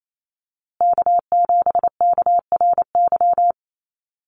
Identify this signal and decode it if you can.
Morse code, heard as a keyed tone, about 28 wpm, 700 Hz. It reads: K7KRY
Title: USA Call Signs Repeated 2x 28wpm